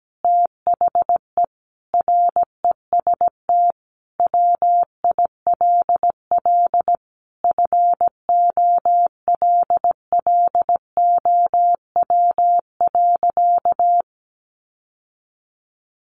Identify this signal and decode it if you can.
Morse code, heard as a keyed tone, about 17 wpm, 700 Hz. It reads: THE REST WILL FOLLOW.
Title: Sentences from Top 1500 Words 17wpm